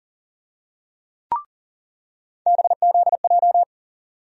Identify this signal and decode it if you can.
Morse code, heard as a keyed tone, about 38 wpm, 700 Hz. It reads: BZJ
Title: USA Call Sign Postfixes - 3 Letters 40wpm